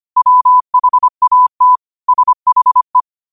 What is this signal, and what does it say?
Morse code, heard as a keyed tone, about 25 wpm, 1000 Hz. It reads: WHAT SHE
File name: qotdfiles / 25wpm